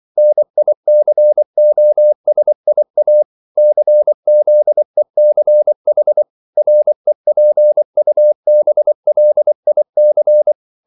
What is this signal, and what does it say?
Morse code, heard as a keyed tone, about 24 wpm, 600 Hz. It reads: NICOSIA CZECH REPUBLIC
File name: CW / Practice / NationalCapitals24